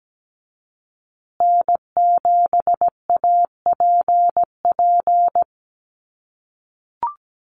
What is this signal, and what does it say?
Morse code, heard as a keyed tone, about 17 wpm, 700 Hz. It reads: N7APP
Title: USA Call Signs Repeated 2x 17wpm